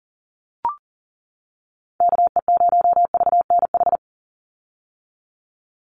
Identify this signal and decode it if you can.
Morse code, heard as a keyed tone, about 37 wpm, 690 Hz. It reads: KE0VNH
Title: USA Call Signs 40wpm